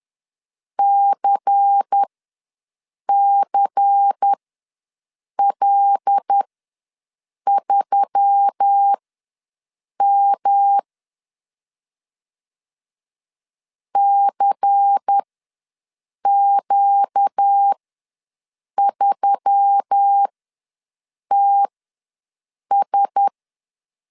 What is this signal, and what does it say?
Morse code, heard as a keyed tone, about 10 wpm, spaced as for 6 wpm, 800 Hz. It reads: CCL3M CQ3TS
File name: LEZIONI / 5 LBVC / 5 LBVC ws 10wpm 3733